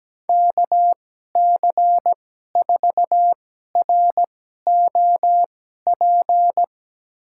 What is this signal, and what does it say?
Morse code, heard as a keyed tone, about 17 wpm, spaced as for 12 wpm, 700 Hz. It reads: KC4ROP
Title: USA Call Signs - 2x Character Spacing - 17wpm